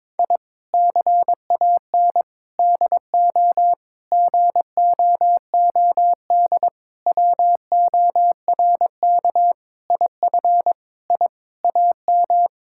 Morse code, heard as a keyed tone, about 21 wpm, 700 Hz. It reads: I CAN DO GOOD WORK IF I AM
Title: Sentences from Top 1000 Words - Review Part 6 - 22wpm